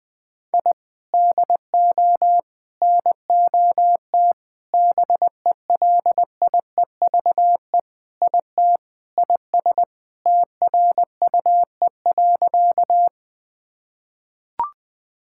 Morse code, heard as a keyed tone, about 20 wpm, 700 Hz. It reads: I DO NOT BELIEVE IT IS TRUE.
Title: Sentences with 5-Word Phrases - 20wpm